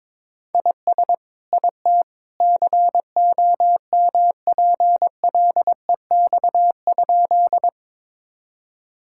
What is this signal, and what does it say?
Morse code, heard as a keyed tone, about 22 wpm, 700 Hz. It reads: IS IT COMPLEX?
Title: Sentences from Top 1000 Words - Review Part 6 - 22wpm